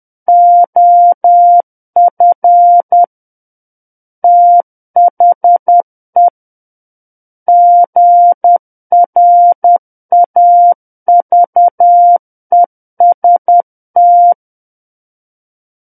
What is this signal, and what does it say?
Morse code, heard as a keyed tone, about 10 wpm, 700 Hz. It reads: OF THE GRAVEST
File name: Book One / 01-Chapter 1 - 10 WPM